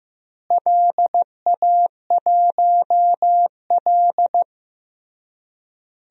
Morse code, heard as a keyed tone, about 15 wpm, 700 Hz. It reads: LA1L
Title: DXCC-100 Call Signs - 4 Characters - Repeated 3x - 15wpm